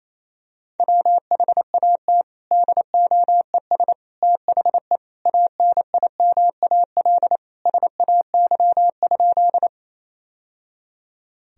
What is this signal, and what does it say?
Morse code, heard as a keyed tone, about 28 wpm, 700 Hz. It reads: WHAT DOES THE ANIMAL SAY?
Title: Sentences from Top 1000 Words - Review Part 1 - 28wpm